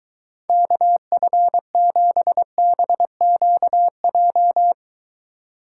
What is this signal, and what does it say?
Morse code, heard as a keyed tone, about 23 wpm, 700 Hz. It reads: KF7BQJ